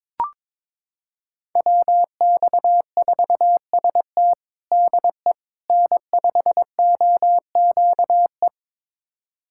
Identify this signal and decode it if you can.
Morse code, heard as a keyed tone, about 22 wpm, 700 Hz. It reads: WX4ST DE N5OQE
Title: USA Callsign DE Callsign 22wpm